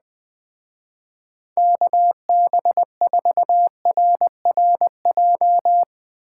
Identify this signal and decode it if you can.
Morse code, heard as a keyed tone, about 20 wpm, 700 Hz. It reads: KB4RRJ
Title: Speed Racing - USA 2x3 Call Signs 17wpm